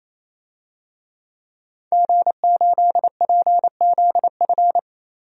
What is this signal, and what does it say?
Morse code, heard as a keyed tone, about 28 wpm, 700 Hz. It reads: G8PZF
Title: DXCC-100 Call Signs - 5 Characters - 28wpm